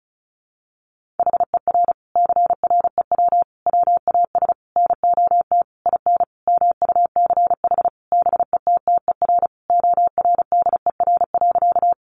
Morse code, heard as a keyed tone, about 34 wpm, 700 Hz. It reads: HER CREW WAS NOT IN MUCH BETTER ORDER.
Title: Sentences from Top 1500 Words 35wpm